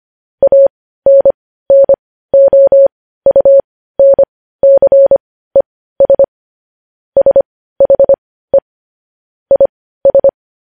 Morse code, heard as a keyed tone, about 25 wpm, spaced as for 15 wpm, 550 Hz. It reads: ANNOUNCES SHE IS
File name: podcast / News headlines-1x-25x15